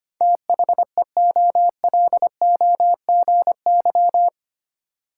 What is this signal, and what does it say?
Morse code, heard as a keyed tone, about 25 wpm, 700 Hz. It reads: THEOLOGY
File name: Eight Letter Words - 25wpm